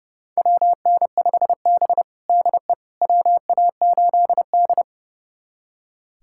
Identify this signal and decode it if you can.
Morse code, heard as a keyed tone, about 30 wpm, 700 Hz. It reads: WN5B DE WA8D
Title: USA Callsign DE Callsign - 4 Characters - 30wpm